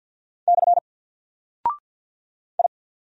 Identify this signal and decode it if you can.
Morse code, heard as a keyed tone, about 48 wpm, 700 Hz. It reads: / I
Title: Single Character (Letter:Number:Punctuation:Prosign) 50wpm